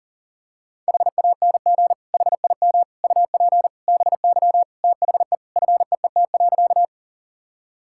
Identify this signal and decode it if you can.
Morse code, heard as a keyed tone, about 40 wpm, 700 Hz. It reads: HANG HIM UP BY THE FEET.
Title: Sentences from Top 1600 Words 40wpm